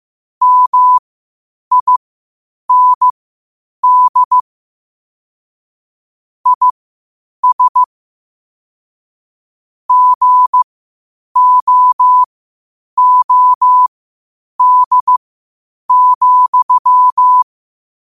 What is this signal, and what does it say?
Morse code, heard as a keyed tone, about 15 wpm, spaced as for 8 wpm, 1000 Hz. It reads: MIND IS GOOD,
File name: qotdfiles / 8wpm